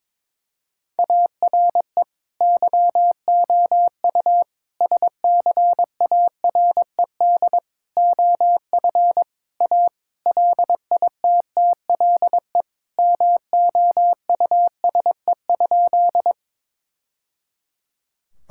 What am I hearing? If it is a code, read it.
Morse code, heard as a keyed tone, about 22 wpm, 700 Hz. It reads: ARE YOU SCARED OF A LITTLE MOUSE?
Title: Sentences from Top 1000 Words - Review Part 5 - 22wpm